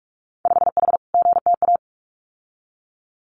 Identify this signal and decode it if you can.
Morse code, heard as a keyed tone, about 46 wpm, 700 Hz. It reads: 5H GTA
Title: Winter Field Day Exchanges Encore 45wpm